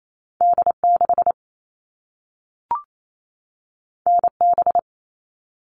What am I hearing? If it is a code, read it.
Morse code, heard as a keyed tone, about 28 wpm, 700 Hz. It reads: D6 NB
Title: All Cut Numbers - 2-Digit 28wpm